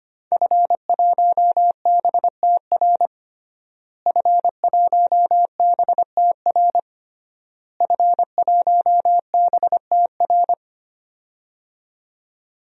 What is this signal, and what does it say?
Morse code, heard as a keyed tone, about 25 wpm, 700 Hz. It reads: F1BTR F1BTR F1BTR
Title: DXCC-100 Call Signs - 5 Characters - Repeated 3x - 25wpm